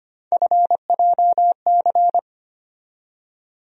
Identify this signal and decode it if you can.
Morse code, heard as a keyed tone, about 25 wpm, 700 Hz. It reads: FJC
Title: USA Call Sign Postfixes - 3 Letters 25wpm